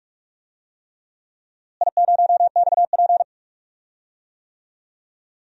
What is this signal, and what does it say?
Morse code, heard as a keyed tone, about 46 wpm, 700 Hz. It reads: I0XP
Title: DXCC-100 Call Signs - 4 Characters - Repeated 3x - 45wpm